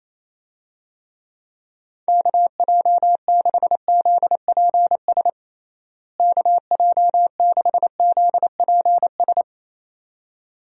Morse code, heard as a keyed tone, about 28 wpm, 700 Hz. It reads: KJ6ZPS KJ6ZPS
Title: USA Call Signs - 6 Characters - Repeated 2x - 28wpm